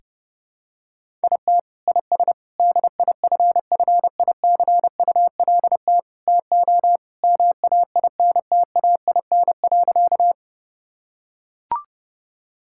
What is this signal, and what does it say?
Morse code, heard as a keyed tone, about 30 wpm, 700 Hz. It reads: IT IS DIFFICULT TO MAINTAIN.
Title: Sentences from Top 1000 Words - Review Part 5 - 30wpm